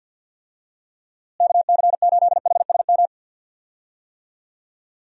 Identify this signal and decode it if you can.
Morse code, heard as a keyed tone, about 51 wpm, 700 Hz. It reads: KC8HSM
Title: USA Call Signs - 6 Characters - Repeated 2x - 50wpm